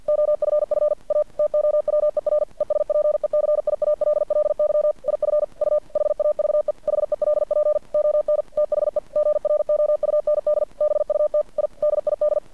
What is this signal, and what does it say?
Morse code, heard as a keyed tone, about 50 wpm, 600 Hz. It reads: OFF A TOWER ESPECIALLY IF U HAVE HELP ON THE GROUND BUT I DID
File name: archived / CRSnet